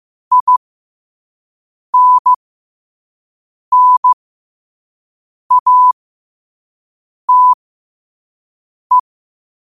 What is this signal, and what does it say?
Morse code, heard as a keyed tone, about 15 wpm, spaced as for 5 wpm, 1000 Hz. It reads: INNATE
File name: qotdfiles / 5wpm